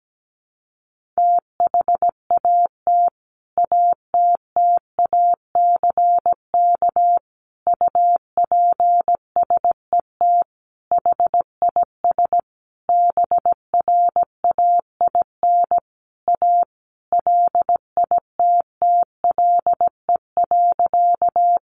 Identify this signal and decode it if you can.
Morse code, heard as a keyed tone, about 17 wpm, 700 Hz. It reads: THAT ATTACK UPSET HIS BRAIN A LITTLE.
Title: Sentences from Top 1500 Words 17wpm